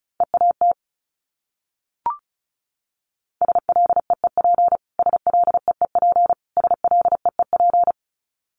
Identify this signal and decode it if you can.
Morse code, heard as a keyed tone, about 34 wpm, 700 Hz. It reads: EAT SLEEP SLEEP SLEEP
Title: Top 500 Words Repeated 35wpm